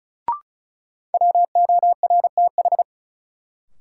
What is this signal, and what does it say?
Morse code, heard as a keyed tone, about 36 wpm, 700 Hz. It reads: WORTH
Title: Sentences from Top 800 Words - New Word Drill - 35wpm